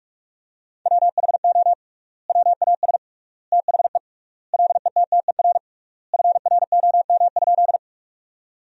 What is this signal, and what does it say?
Morse code, heard as a keyed tone, about 46 wpm, 700 Hz. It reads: WHO WAS THE LETTER FROM?
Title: 3x Word-Spacing - Sentences from Top 300 Words - 45wpm